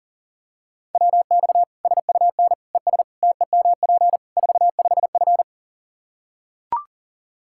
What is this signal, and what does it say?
Morse code, heard as a keyed tone, about 40 wpm, 700 Hz. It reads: WX SUN ES TEMP 45F
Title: Practice QSO - Exchange 2 - Info - 40wpm